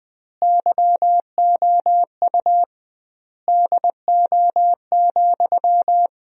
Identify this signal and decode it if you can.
Morse code, heard as a keyed tone, about 20 wpm, 700 Hz. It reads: YOU DO,
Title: 2x Word-Spacing - Sentences from Top 100 Words - 20wpm